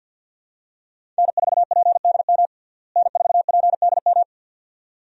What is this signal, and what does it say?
Morse code, heard as a keyed tone, about 48 wpm, 700 Hz. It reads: N4PDM N4PDM
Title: USA Call Signs - 5 Characters - Repeated 2x - 50wpm